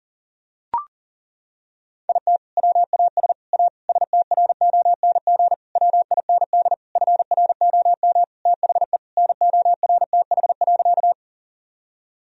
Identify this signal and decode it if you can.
Morse code, heard as a keyed tone, about 40 wpm, 700 Hz. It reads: IT WAS A STRONG WIND FROM THE NORTH.
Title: Sentences from Top 1000 Words - Review Part 2 - 40wpm